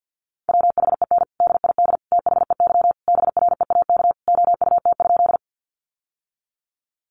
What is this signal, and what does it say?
Morse code, heard as a keyed tone, about 50 wpm, 700 Hz. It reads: WHEN DID THEY BREAK OUT?
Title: Sentences from Top 900 Words 50wpm